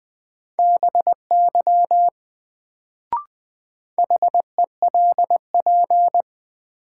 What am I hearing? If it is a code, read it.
Morse code, heard as a keyed tone, about 20 wpm, 700 Hz. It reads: BY HELP
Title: Sets of 2 Words - New Word Drill - 20wpm